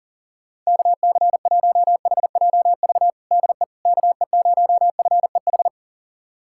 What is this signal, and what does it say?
Morse code, heard as a keyed tone, about 40 wpm, 700 Hz. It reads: KC1HJV DE KE0FEH
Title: USA Callsign DE Callsign 40wpm